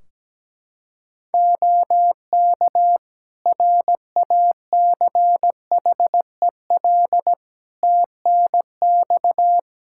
Morse code, heard as a keyed tone, about 17 wpm, 700 Hz. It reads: OK RACHEL TNX
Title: K1USN SST QSO Practice 17wpm